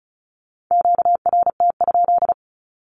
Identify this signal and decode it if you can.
Morse code, heard as a keyed tone, about 34 wpm, 700 Hz. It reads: QRT?
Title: Q-Codes 35wpm